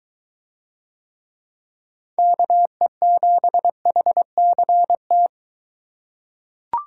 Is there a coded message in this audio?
Yes: Morse code from 2.2 to 5.3 s, heard as a keyed tone, about 22 wpm, 700 Hz. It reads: KE7HCT